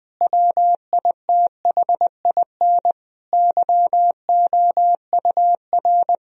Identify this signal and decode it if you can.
Morse code, heard as a keyed tone, about 20 wpm, 700 Hz. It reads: WITHIN YOUR
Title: Sentences from Top 1000 Words - Review Part 4 - 20wpm